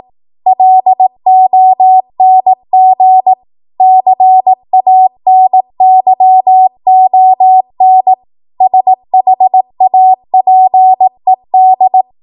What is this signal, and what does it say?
Morse code, heard as a keyed tone, about 18 wpm, 750 Hz. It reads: LONG CANYON SHAPED